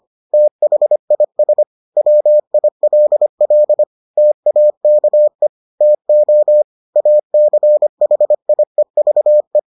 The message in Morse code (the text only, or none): THIS WILL TAKE TO ACHIEVE